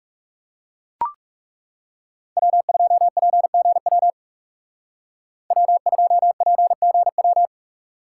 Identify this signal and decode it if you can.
Morse code, heard as a keyed tone, about 39 wpm, 700 Hz. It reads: W2PGW W2PGW